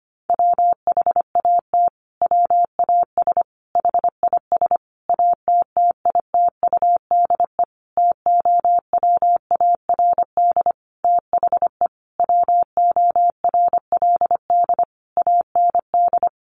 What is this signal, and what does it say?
Morse code, heard as a keyed tone, about 25 wpm, 700 Hz. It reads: WHAT WAS HIS ATTITUDE TOWARD THE WORLD AND
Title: Sentences from Top 1300 Words 25wpm